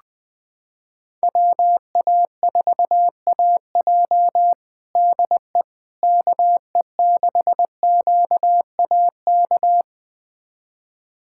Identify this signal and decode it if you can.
Morse code, heard as a keyed tone, about 20 wpm, 700 Hz. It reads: WA4AJ DE KE6QAK
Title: USA Callsign DE Callsign 20wpm